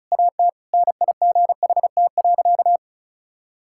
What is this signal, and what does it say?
Morse code, heard as a keyed tone, about 36 wpm, 700 Hz. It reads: AT NIGHT.
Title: Sentences from Top 1000 Words - Review Part 6 - 35wpm